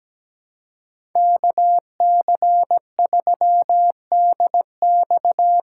Morse code, heard as a keyed tone, about 17 wpm, 700 Hz. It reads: KC3DX
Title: USA Call Signs - 5 Characters - 3x Character Spacing - 17wpm